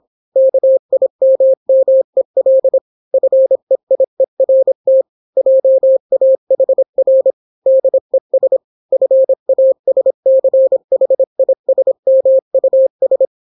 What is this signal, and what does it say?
Morse code, heard as a keyed tone, about 26 wpm, 530 Hz. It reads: KIMMEL FEIERT JAHR DES FASCHISMUS